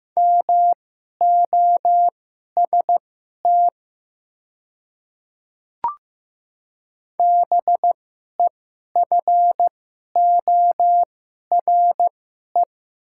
Morse code, heard as a keyed tone, about 15 wpm, spaced as for 11 wpm, 700 Hz. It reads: MOST BEFORE
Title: 2x Character Spacing - Top 200 Words - 15wpm